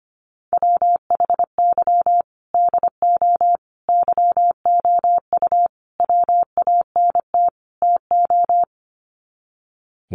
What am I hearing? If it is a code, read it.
Morse code, heard as a keyed tone, about 25 wpm, 700 Hz. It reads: WHY DO YOU WANT TO
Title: Sets of 5 Words Encore2 25wpm